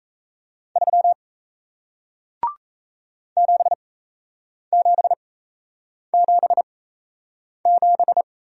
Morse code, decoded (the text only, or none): 3 7777